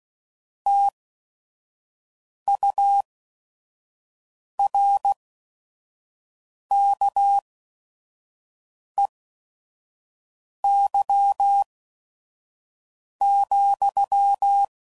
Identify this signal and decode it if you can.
Morse code, heard as a keyed tone, about 16 wpm, spaced as for 5 wpm, 800 Hz. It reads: TURKEY,
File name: Mixed / Misc / WordPairs